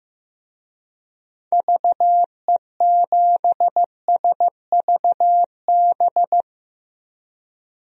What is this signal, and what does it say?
Morse code, heard as a keyed tone, about 15 wpm, 700 Hz. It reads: VE7SVB
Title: DXCC-100 Call Signs - 6 Characters - Repeated 2x - 15wpm